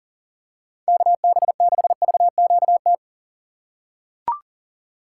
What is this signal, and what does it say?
Morse code, heard as a keyed tone, about 40 wpm, 700 Hz. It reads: KB6VQT
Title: USA Call Signs - 6 Characters - 6x Character Spacing - 40wpm